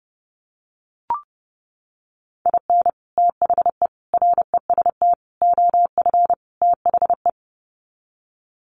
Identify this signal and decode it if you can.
Morse code, heard as a keyed tone, about 30 wpm, 700 Hz. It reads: IN THE REST OF THE
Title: Sets of 5 Words 30wpm